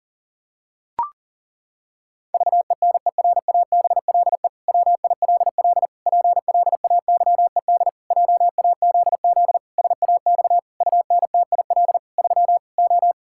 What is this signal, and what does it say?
Morse code, heard as a keyed tone, about 40 wpm, 700 Hz. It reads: VENERABLE WILL PLAYED JAZZ SAX UNTIL 3 O
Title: Pangram Challenge v2 - 40wpm